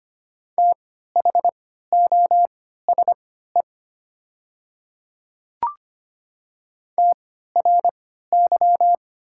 Morse code, heard as a keyed tone, about 25 wpm, spaced as for 14 wpm, 700 Hz. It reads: THOSE TRY
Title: 3x Character Spacing - Top 300 Words - 25wpm